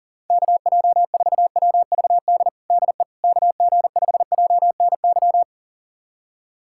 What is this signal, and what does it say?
Morse code, heard as a keyed tone, about 40 wpm, 700 Hz. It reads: KJ4WVD DE KG5JNY
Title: USA Callsign DE Callsign - 6 Characters - 40wpm